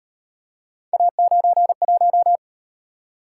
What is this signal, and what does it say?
Morse code, heard as a keyed tone, about 37 wpm, 700 Hz. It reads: A91